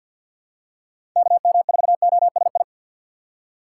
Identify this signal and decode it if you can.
Morse code, heard as a keyed tone, about 47 wpm, 700 Hz. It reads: KM4OSI
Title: USA Call Signs - 6x Character Spacing - 50wpm